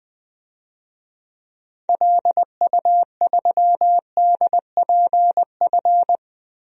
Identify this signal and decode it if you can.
Morse code, heard as a keyed tone, about 20 wpm, 700 Hz. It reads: LU3DPF
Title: DXCC-100 Call Signs - 6 Characters - 20wpm